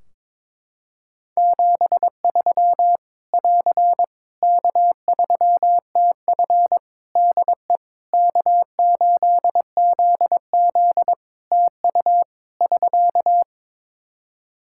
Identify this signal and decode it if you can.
Morse code, heard as a keyed tone, about 22 wpm, 700 Hz. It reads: 73 <AR> K3TF DE K8ZZ TU <SK>
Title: K1USN SST QSO Practice 22wpm